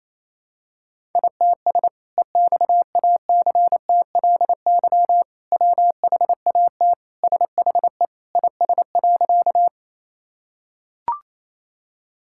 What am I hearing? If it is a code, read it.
Morse code, heard as a keyed tone, about 28 wpm, 700 Hz. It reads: ITS EXACTLY WHAT SHE IS.
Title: Sentences from Top 1000 Words - Review Part 6 - 28wpm